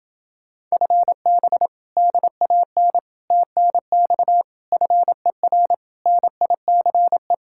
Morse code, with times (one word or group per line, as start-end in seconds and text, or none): 0.7-1.7 FB
2.0-3.0 DAN
3.3-4.4 TNX
4.7-5.7 FER
6.1-7.3 NICE